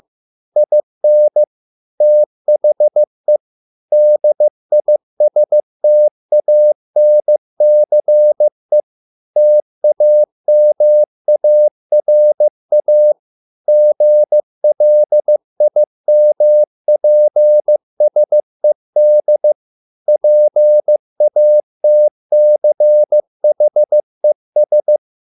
Morse code, heard as a keyed tone, about 15 wpm, 600 Hz. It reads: IN THE DISTANCE TAMARA GLIMPSED PATCHES